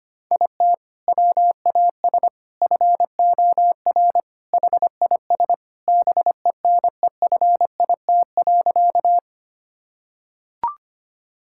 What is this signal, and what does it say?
Morse code, heard as a keyed tone, about 25 wpm, 700 Hz. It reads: IT WAS FOR HIS BENEFIT.
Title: Sentences from Top 1000 Words - Review Part 6 - 25wpm